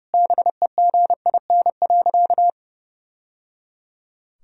Morse code, heard as a keyed tone, about 30 wpm, 700 Hz. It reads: BEGIN.
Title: Sentences from Top 1000 Words - Review Part 1 - 30wpm